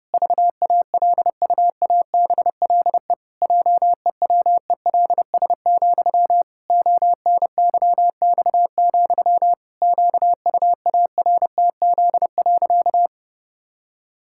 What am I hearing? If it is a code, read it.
Morse code, heard as a keyed tone, about 32 wpm, 700 Hz. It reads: VALUABLE JEWELS, ONYX, QUARTZ.
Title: Pangram Challenge v2 - 30wpm